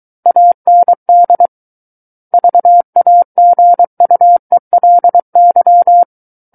Morse code, heard as a keyed tone, about 23 wpm, 700 Hz. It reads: AND VAGUELY
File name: Book One / 14-Chapter 14 - 23 WPM